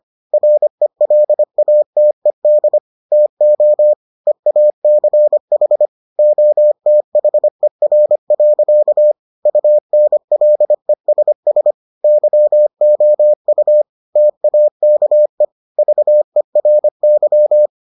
Morse code, heard as a keyed tone, about 25 wpm, 600 Hz. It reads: RELATED TO EACH OTHER. UNLESS YOU TAKE VERY